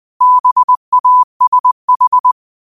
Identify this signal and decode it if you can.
Morse code, heard as a keyed tone, about 20 wpm, 1000 Hz. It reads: BASH
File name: qotdfiles / 20wpm